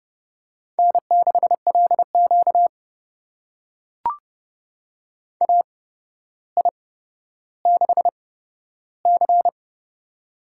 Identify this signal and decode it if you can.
Morse code, heard as a keyed tone, about 30 wpm, spaced as for 13 wpm, 700 Hz. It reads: N6LQ AI6C
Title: USA Call Signs - 4 Characters - 8x Character Spacing - 30wpm